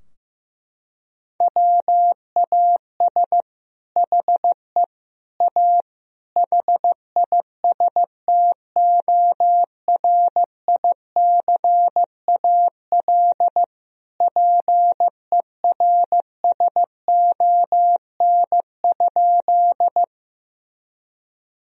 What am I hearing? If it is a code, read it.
Morse code, heard as a keyed tone, about 15 wpm, 700 Hz. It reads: WAS HE A HISTORICAL PERSON?
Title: Sentences from Top 1000 Words - Review Part 4 - 15wpm